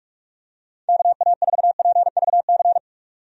Morse code, heard as a keyed tone, about 46 wpm, 700 Hz. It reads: KA4PVC
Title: USA Call Signs - 6 Characters - 45wpm